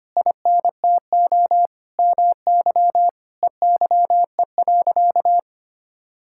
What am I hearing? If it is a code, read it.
Morse code, heard as a keyed tone, about 25 wpm, 700 Hz. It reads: INTO MY EYE.
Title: Sentences from Top 1600 Words 25wpm